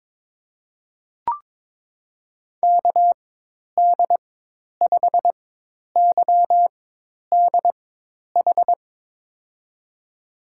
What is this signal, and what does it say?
Morse code, heard as a keyed tone, about 21 wpm, spaced as for 10 wpm, 700 Hz. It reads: KD5YDH
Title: USA Call Signs - 6 Characters - 4x Character Spacing - 22wpm